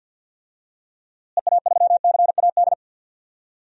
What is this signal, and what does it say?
Morse code, heard as a keyed tone, about 49 wpm, 700 Hz. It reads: EA3CAD
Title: DXCC-100 Call Signs - 6 Characters - Repeated 3x - 50wpm